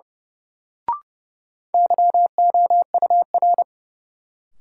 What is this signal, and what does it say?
Morse code, heard as a keyed tone, about 30 wpm, 700 Hz. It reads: YOUR
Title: Sentences from Top 100 Words - New Word Drill - 30wpm